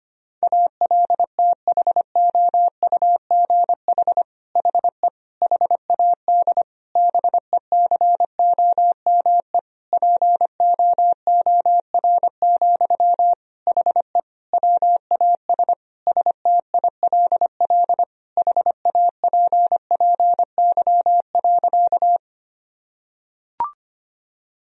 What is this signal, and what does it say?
Morse code, heard as a keyed tone, about 25 wpm, 700 Hz. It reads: ALTHOUGH HE HAD BECOME POOR, HE WAS STILL HAPPY.
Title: Sentences from Top 1000 Words - Review Part 3 - 25wpm